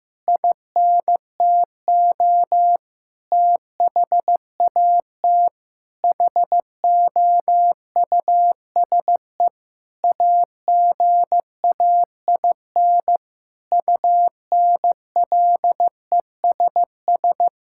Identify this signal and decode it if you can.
Morse code, heard as a keyed tone, about 15 wpm, 700 Hz. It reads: INTO THAT HOUSE AGAIN UNLESS
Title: Sentences from Top 1000 Words - Review Part 6 - 15wpm